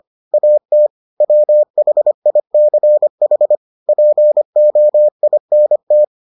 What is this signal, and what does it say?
Morse code, heard as a keyed tone, about 25 wpm, 600 Hz. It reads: AT WHICH POINT